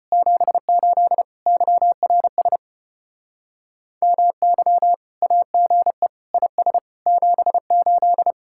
Morse code, heard as a keyed tone, about 33 wpm, 700 Hz. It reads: 78 YRS MY AGE IS 78